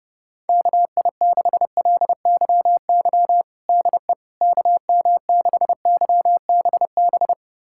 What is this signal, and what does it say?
Morse code, heard as a keyed tone, about 32 wpm, 700 Hz. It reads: KI6LYY DE KM6YBB
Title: USA Callsign DE Callsign - 6 Characters - 30wpm